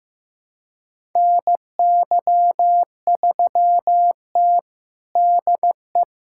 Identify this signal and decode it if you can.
Morse code, heard as a keyed tone, about 15 wpm, 700 Hz. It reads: NY3T DE
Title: USA Callsign DE Callsign - 4 Characters - 15wpm